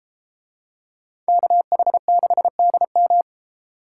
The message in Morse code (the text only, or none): KH6DM